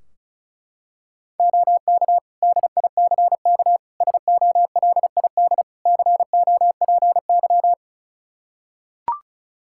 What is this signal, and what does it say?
Morse code, heard as a keyed tone, about 36 wpm, 700 Hz. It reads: OK DICK SOLID COPY
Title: Practice QSO - Exchange 3 - More Info - 35wpm